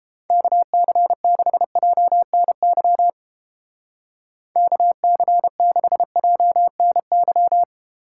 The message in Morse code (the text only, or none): KC6JNY KC6JNY